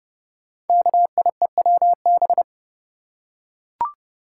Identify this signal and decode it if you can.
Morse code, heard as a keyed tone, about 31 wpm, 700 Hz. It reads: KIEWB
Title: Code Groups - 5 Letters - 30wpm